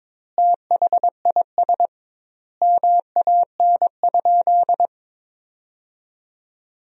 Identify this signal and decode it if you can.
Morse code, heard as a keyed tone, about 22 wpm, 700 Hz. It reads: THIS MAN?
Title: 2x Word-Spacing - Sentences from Top 200 Words - 22wpm